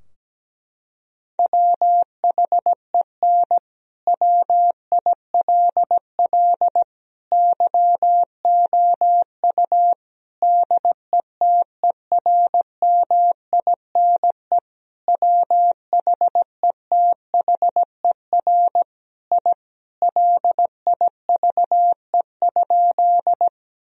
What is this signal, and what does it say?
Morse code, heard as a keyed tone, about 17 wpm, 700 Hz. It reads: WHEN WILL YOU DETERMINE WHETHER I LIVE?
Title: Sentences from Top 1000 Words - Review Part 4 - 17wpm